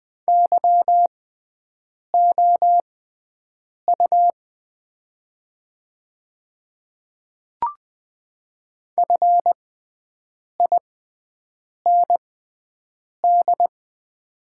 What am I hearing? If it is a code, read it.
Morse code, heard as a keyed tone, about 20 wpm, spaced as for 7 wpm, 700 Hz. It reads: YOU FIND
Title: 6x Character Spacing - Top 200 Words - 20wpm